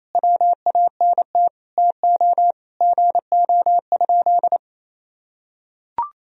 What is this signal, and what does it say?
Morse code, heard as a keyed tone, about 28 wpm, 700 Hz. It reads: WANT TO GO?
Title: Sentences from Top 300 Words and Abbreviations 28wpm